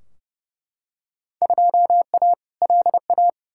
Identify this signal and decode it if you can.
Morse code, heard as a keyed tone, about 30 wpm, 700 Hz. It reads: 2A LA
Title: Field Day Exchanges - 30wpm